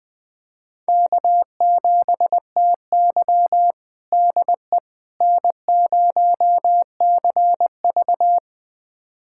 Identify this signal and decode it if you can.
Morse code, heard as a keyed tone, about 20 wpm, 700 Hz. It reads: K7TY DE N0CV
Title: USA Callsign DE Callsign - 4 Characters - 20wpm